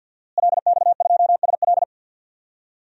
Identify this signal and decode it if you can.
Morse code, heard as a keyed tone, about 52 wpm, 700 Hz. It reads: LX2SL
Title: DXCC-100 Call Signs - 5 Characters - Repeated 3x - 50wpm